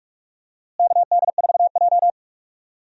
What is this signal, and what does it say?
Morse code, heard as a keyed tone, about 48 wpm, 700 Hz. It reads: KD4J
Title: USA Call Signs - 4 Characters - Repeated 2x - 45wpm